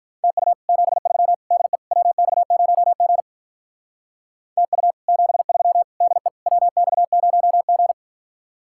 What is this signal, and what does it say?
Morse code, heard as a keyed tone, about 43 wpm, 700 Hz. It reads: TU 73 DE WX0G TU 73 DE WX0G
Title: Speed-Racing Common QSO Phrases - 35wpm